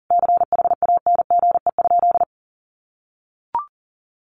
Morse code, heard as a keyed tone, about 40 wpm, 700 Hz. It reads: CHANGE?
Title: Sentences from Top 1500 Words 40wpm